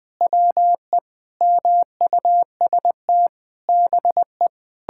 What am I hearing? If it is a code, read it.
Morse code, heard as a keyed tone, about 20 wpm, 700 Hz. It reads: WE MUST BE
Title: Sentences with 8-Word Phrases - 20wpm